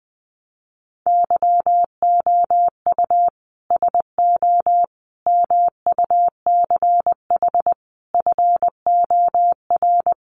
Morse code, heard as a keyed tone, about 19 wpm, 700 Hz. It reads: YOU SO MUCH FOR
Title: Sets of 5 Words Encore2 20wpm